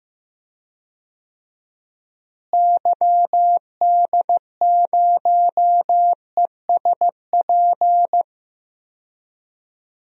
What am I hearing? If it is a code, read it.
Morse code, heard as a keyed tone, about 15 wpm, 700 Hz. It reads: YD0ESP
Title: DXCC-100 Call Signs - 6 Characters - Repeated 2x - 15wpm